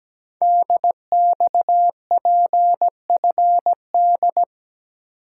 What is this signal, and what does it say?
Morse code, heard as a keyed tone, about 17 wpm, 700 Hz. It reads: DXPFD
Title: Code Groups - 5 Letters - 17wpm